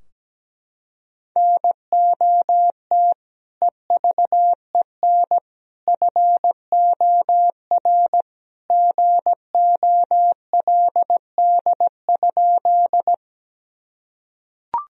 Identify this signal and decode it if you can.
Morse code, heard as a keyed tone, about 17 wpm, 700 Hz. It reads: NOT EVEN FOR GOLD?
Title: Sentences from Top 1000 Words - Review Part 3 - 17wpm